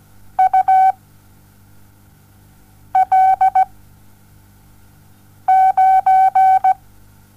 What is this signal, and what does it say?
Morse code, heard as a keyed tone, about 17 wpm, 760 Hz. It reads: U L 9
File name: Onlinelehrgang / morsen / DJ4UF Morsen07